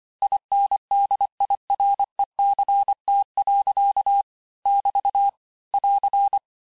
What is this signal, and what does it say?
Morse code, heard as a keyed tone, about 25 wpm, 800 Hz. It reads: INDIRECT. <BT> <AR>
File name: cours-du-lundi files / cours-1595-teste-CLAIR-vitesse18-MPM-puis23MPM